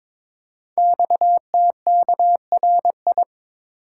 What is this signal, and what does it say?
Morse code, heard as a keyed tone, about 23 wpm, 700 Hz. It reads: XTKRI
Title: Code Groups - 5 Letters - 22wpm